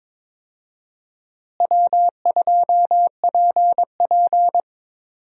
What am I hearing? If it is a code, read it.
Morse code, heard as a keyed tone, about 22 wpm, 700 Hz. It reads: W2PP
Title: USA Call Signs - 4 Characters - Repeated 3x - 22wpm